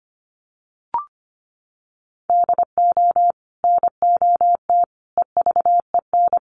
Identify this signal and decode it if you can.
Morse code, heard as a keyed tone, about 25 wpm, 700 Hz. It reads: DO NOT EVEN
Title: Sets of 4 Words 25wpm